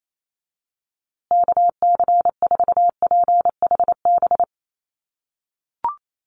Morse code, heard as a keyed tone, about 28 wpm, 700 Hz. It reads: KC4PHB
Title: USA Call Signs Repeated 2x 28wpm